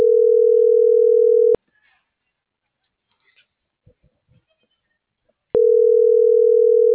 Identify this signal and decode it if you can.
Ringback tone (North America), 440 + 480 Hz.
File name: CENTRAL MABE / out antonio.garcia - .95424